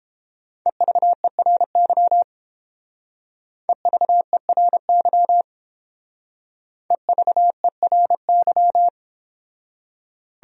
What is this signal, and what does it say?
Morse code, heard as a keyed tone, about 31 wpm, 700 Hz. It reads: EVERY EVERY EVERY